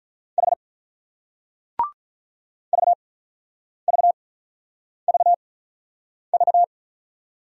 Morse code, heard as a keyed tone, about 46 wpm, spaced as for 9 wpm, 700 Hz. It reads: H VVVV